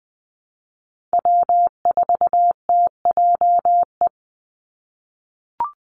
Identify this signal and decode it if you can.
Morse code, heard as a keyed tone, about 20 wpm, 700 Hz. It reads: W4TJE
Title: Call Signs - 20wpm